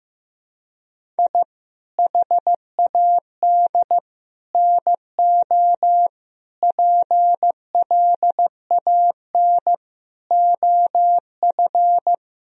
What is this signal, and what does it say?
Morse code, heard as a keyed tone, about 15 wpm, 700 Hz. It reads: I HAD NO PLAN OF
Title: Sentences from Top 1000 Words - Review Part 2 - 15wpm